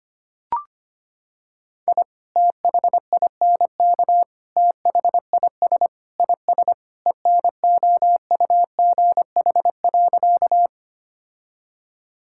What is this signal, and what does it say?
Morse code, heard as a keyed tone, about 25 wpm, 700 Hz. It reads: I THINK THIS IS ENOUGH.
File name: practice / 25wpm / Sentences with 4-Word Phrases - 25wpm